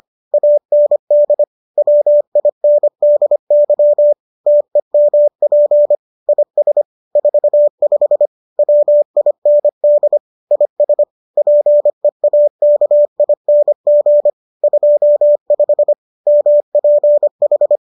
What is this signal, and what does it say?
Morse code, heard as a keyed tone, about 25 wpm, 590 Hz. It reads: AND WINDY TEMP IS 45 WIND IS PEAKING 25 MPH
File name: qso206 25